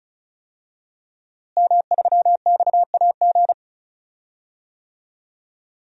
Morse code, heard as a keyed tone, about 36 wpm, 700 Hz. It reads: M3XAG
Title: DXCC-100 Call Signs - 5 Characters - Repeated 3x - 35wpm